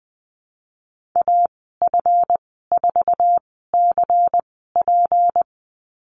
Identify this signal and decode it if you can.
Morse code, heard as a keyed tone, about 20 wpm, spaced as for 15 wpm, 700 Hz. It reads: AF4CP
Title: 173 Lesson 28g - Number 9 - Call Signs 20wpm